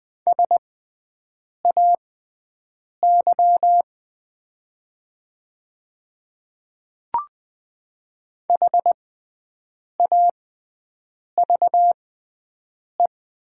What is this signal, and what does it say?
Morse code, heard as a keyed tone, about 20 wpm, spaced as for 7 wpm, 700 Hz. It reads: SAY HAVE
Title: 6x Character Spacing - Top 300 Words - 20wpm